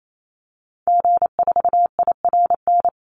Morse code, heard as a keyed tone, about 28 wpm, 700 Hz. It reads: G4IRN
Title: Call Signs Repeated 3x - 28wpm